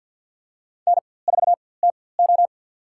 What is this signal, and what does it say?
Morse code, heard as a keyed tone, about 47 wpm, spaced as for 23 wpm, 700 Hz. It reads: N4TO
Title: USA Call Signs - 4x Character Spacing - 50wpm